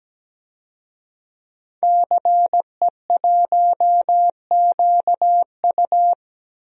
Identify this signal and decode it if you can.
Morse code, heard as a keyed tone, about 17 wpm, 700 Hz. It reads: CE1QU
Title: DXCC-100 Call Signs - 5 Characters - 17wpm